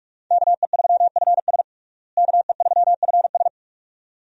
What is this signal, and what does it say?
Morse code, heard as a keyed tone, about 46 wpm, 700 Hz. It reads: KE3FS KE3FS
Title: USA Call Signs - 5 Characters - Repeated 2x - 45wpm